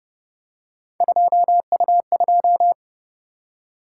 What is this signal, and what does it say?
Morse code, heard as a keyed tone, about 30 wpm, 700 Hz. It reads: 2U2